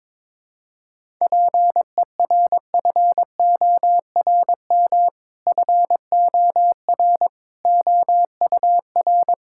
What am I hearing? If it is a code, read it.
Morse code, heard as a keyed tone, about 21 wpm, 700 Hz. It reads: PERFORM FOR OUR
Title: Sentences from Top 1000 Words - Review Part 5 - 22wpm